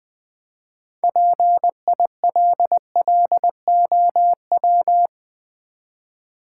Morse code, heard as a keyed tone, about 20 wpm, 700 Hz. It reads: PILLOW
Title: Easy Words with Hints - 20wpm